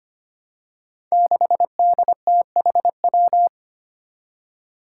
Morse code, heard as a keyed tone, about 25 wpm, 700 Hz. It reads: 6DTHW
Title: Five Challenging Letter-Number Combinations 25wpm